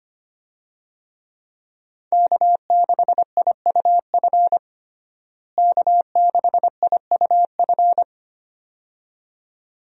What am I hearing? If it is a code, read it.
Morse code, heard as a keyed tone, about 25 wpm, 700 Hz. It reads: K6IUF K6IUF
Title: USA Call Signs - 5 Characters - Repeated 2x - 25wpm